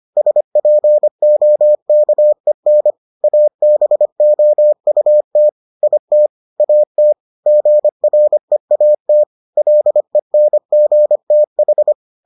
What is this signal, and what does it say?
Morse code, heard as a keyed tone, about 25 wpm, 600 Hz. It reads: SPOKEN ABOUT IT AT GREAT LENGTH